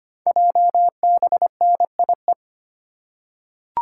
Morse code, heard as a keyed tone, about 25 wpm, 700 Hz. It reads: JBNIE
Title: Code Groups - 5 Letters - 25wpm